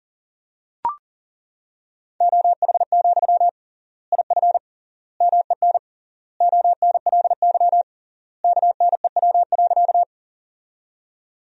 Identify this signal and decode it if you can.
Morse code, heard as a keyed tone, about 40 wpm, 700 Hz. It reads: OH, IF MEN ONLY KNEW.
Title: 3x Word-Spacing - Sentences from Top 500 Words - 40wpm